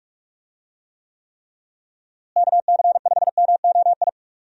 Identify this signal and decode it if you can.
Morse code, heard as a keyed tone, about 43 wpm, 700 Hz. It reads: KC5MOI
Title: USA Call Signs - 6 Characters - Repeated 3x - 45wpm